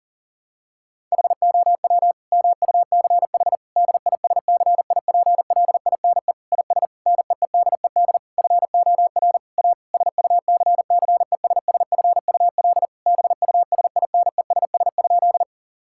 Morse code, heard as a keyed tone, about 37 wpm, 700 Hz. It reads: HOW MUCH DISCIPLINE IS NEEDED FOR A SUCCESSFUL BUSINESS?
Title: Sentences from Top 1000 Words - Review Part 6 - 40wpm